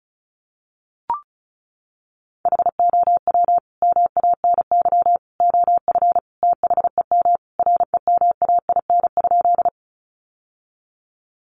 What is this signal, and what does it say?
Morse code, heard as a keyed tone, about 34 wpm, 700 Hz. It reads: HOW MANY OF THEM REMAIN?
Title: Sentences from Top 1200 Words 35wpm